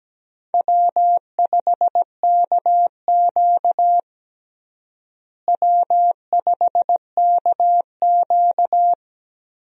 Morse code, heard as a keyed tone, about 17 wpm, 700 Hz. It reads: W5KQ W5KQ
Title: USA Call Signs - 4 Characters - Repeated 3x - 17wpm